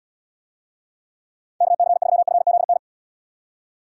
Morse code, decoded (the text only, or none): KC3FZN